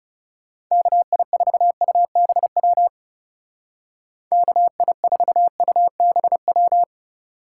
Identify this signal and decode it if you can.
Morse code, heard as a keyed tone, about 38 wpm, 700 Hz. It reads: KI4UBW KI4UBW